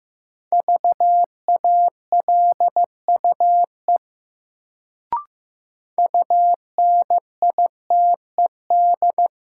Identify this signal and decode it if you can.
Morse code, heard as a keyed tone, about 15 wpm, 700 Hz. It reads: VALUE UNITED
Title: Sentences from Top 600 Words - New Word Drill - 15wpm